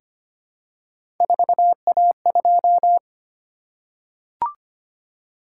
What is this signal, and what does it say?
Morse code, heard as a keyed tone, about 25 wpm, 700 Hz. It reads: 4A2
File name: Three Letter-Number Combinations - 25wpm